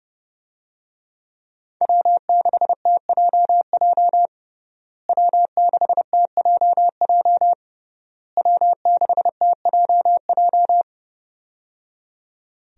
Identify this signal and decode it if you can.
Morse code, heard as a keyed tone, about 30 wpm, 700 Hz. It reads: W6TJJ W6TJJ W6TJJ
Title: USA Call Signs - 5 Characters - Repeated 3x - 30wpm